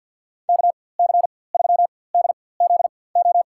Morse code, heard as a keyed tone, about 52 wpm, spaced as for 23 wpm, 700 Hz. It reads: KC3DZO
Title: USA Call Signs - 6 Characters - 4x Character Spacing - 50wpm